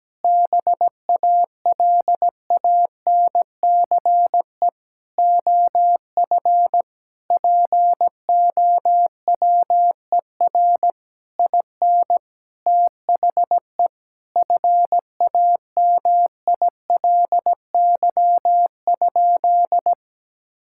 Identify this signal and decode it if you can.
Morse code, heard as a keyed tone, about 17 wpm, 700 Hz. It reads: BALANCE OF POWER IN THE FAMILY?
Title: Sentences from Top 1000 Words - Review Part 6 - 17wpm